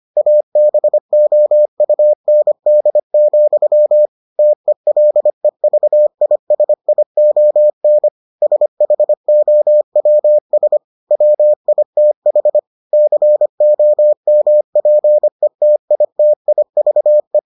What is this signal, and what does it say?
Morse code, heard as a keyed tone, about 25 wpm, 600 Hz. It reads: ABOUND, TELEVISION SHOWS WITH COMPETITIVE